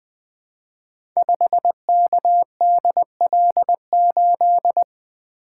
Morse code, heard as a keyed tone, about 20 wpm, 700 Hz. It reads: 5KDL8
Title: Code Groups - 5 Letter:Numbers - 20wpm